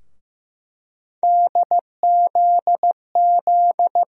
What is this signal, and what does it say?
Morse code, heard as a keyed tone, about 15 wpm, 700 Hz. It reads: DZZ
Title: USA Call Sign Postfixes - 3 Letters 15wpm